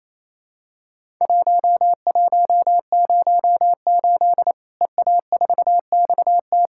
Morse code, heard as a keyed tone, about 28 wpm, 700 Hz. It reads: 1108 EA4XT
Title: Worked All Europe DX Contest - QTC Exchange - 28wpm